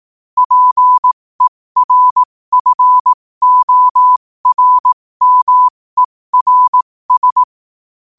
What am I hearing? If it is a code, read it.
Morse code, heard as a keyed tone, about 18 wpm, spaced as for 15 wpm, 1000 Hz. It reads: PERFORMERS
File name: qotdfiles / 15wpm